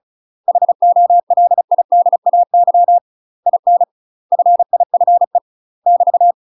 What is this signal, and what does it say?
Morse code, heard as a keyed tone, about 34 wpm, 700 Hz. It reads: HOLIDAY IN FIFE <BT>
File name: UK MASTER QSOs TRAINING / 35WPM / UK MASTER QSOs 0034